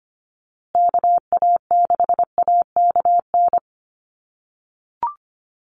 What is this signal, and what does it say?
Morse code, heard as a keyed tone, about 25 wpm, 700 Hz. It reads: KA6AKN
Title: USA Call Signs Repeated 3x 25wpm